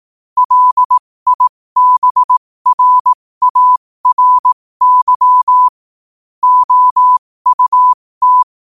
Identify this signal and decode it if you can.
Morse code, heard as a keyed tone, about 18 wpm, spaced as for 15 wpm, 1000 Hz. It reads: LIBRARY OUT
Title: 15 WPM morse code quotes for Thu, 29 Feb 2024